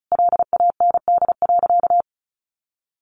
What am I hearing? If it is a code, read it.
Morse code, heard as a keyed tone, about 34 wpm, 700 Hz. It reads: LAND.
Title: Sentences from Top 1100 Words 35wpm